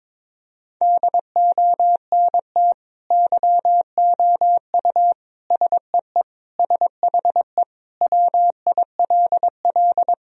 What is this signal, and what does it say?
Morse code, heard as a keyed tone, about 22 wpm, 700 Hz. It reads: DONT YOU SEE SHE WILL
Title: Sentences from Top 1000 Words - Review Part 2 - 22wpm